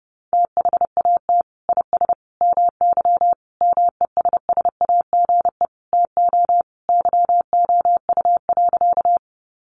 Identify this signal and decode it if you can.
Morse code, heard as a keyed tone, about 30 wpm, 700 Hz. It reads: THAT IS MY MESSAGE TO YOU.
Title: Sentences from Top 1100 Words 30wpm